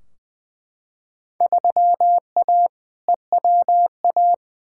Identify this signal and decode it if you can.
Morse code, heard as a keyed tone, about 20 wpm, 700 Hz. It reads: 3A EWA
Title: Field Day Exchanges - 20wpm